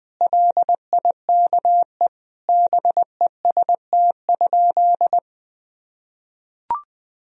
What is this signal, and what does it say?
Morse code, heard as a keyed tone, about 20 wpm, 700 Hz. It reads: LIKE BEST?
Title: Sentences from Top 1000 Words - Review Part 2 - 20wpm